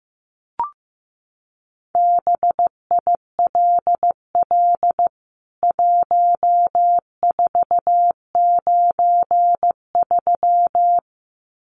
Morse code, heard as a keyed tone, about 14 wpm, 700 Hz. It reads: BILL 1493
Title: CWT Exchange 15wpm